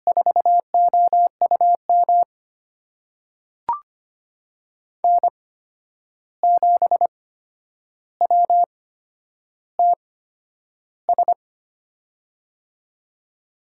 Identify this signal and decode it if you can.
Morse code, heard as a keyed tone, about 25 wpm, spaced as for 10 wpm, 700 Hz. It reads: 4OUM N7WTS
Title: USA Call Signs - 5 Characters - 8x Character Spacing - 25wpm